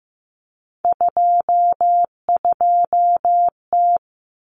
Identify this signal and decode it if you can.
Morse code, heard as a keyed tone, about 15 wpm, 700 Hz. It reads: 22T
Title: Common Cut Numbers - 3-Digit 15wpm